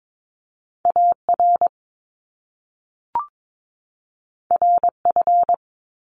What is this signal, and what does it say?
Morse code, heard as a keyed tone, about 22 wpm, 700 Hz. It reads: AR RF
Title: Top Two Letter Combinations in Words 22wpm